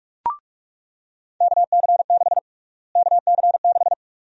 Morse code, heard as a keyed tone, about 46 wpm, 700 Hz. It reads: KC6 KC6
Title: USA Call Sign Prefixes - Repeated 2x - 45wpm